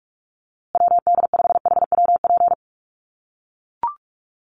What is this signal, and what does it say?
Morse code, heard as a keyed tone, about 44 wpm, 700 Hz. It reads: WD5HWP
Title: USA Call Signs Repeated 2x 45wpm